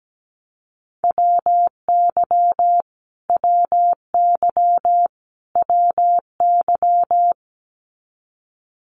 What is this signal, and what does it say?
Morse code, heard as a keyed tone, about 17 wpm, 700 Hz. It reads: WY WY WY
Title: US States Repeated 3x 17wpm